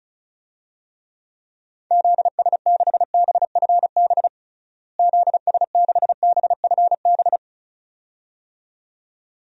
Morse code, heard as a keyed tone, about 34 wpm, 700 Hz. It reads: ZS6BFB ZS6BFB
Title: DXCC-100 Call Signs - 6 Characters - Repeated 2x - 35wpm